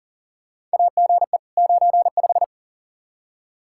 Morse code, heard as a keyed tone, about 40 wpm, 700 Hz. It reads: AGE 95
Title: K1USN SST QSO Practice 40wpm